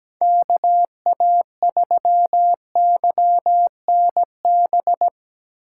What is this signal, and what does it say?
Morse code, heard as a keyed tone, about 18 wpm, 700 Hz. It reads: KA3YNB
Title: Speed Racing - USA 2x3 Call Signs 17wpm